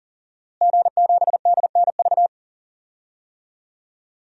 Morse code, heard as a keyed tone, about 40 wpm, 700 Hz. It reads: G7DNV
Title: DXCC-100 Call Signs - 5 Characters - Repeated 3x - 40wpm